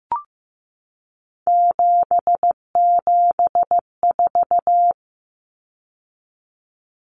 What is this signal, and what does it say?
Morse code, heard as a keyed tone, about 15 wpm, 700 Hz. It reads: 774
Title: Common Cut Numbers - 3-Digit 15wpm